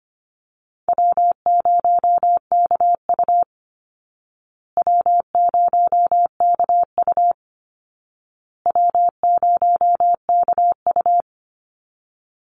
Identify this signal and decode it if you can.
Morse code, heard as a keyed tone, about 25 wpm, 700 Hz. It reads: W0KU W0KU W0KU
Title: Easy Call Signs Repeated 3x - 25wpm